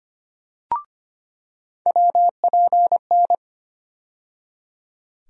Morse code, heard as a keyed tone, about 25 wpm, 700 Hz. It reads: WPN
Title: USA Call Sign Postfixes - 3 Letters 25wpm